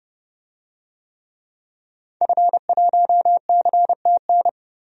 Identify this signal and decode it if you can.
Morse code, heard as a keyed tone, about 31 wpm, 700 Hz. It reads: F1CTN
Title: DXCC-100 Call Signs - 5 Characters - 30wpm